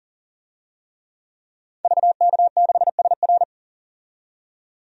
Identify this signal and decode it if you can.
Morse code, heard as a keyed tone, about 40 wpm, 700 Hz. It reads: VK6SR
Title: DXCC-100 Call Signs - 5 Characters - 40wpm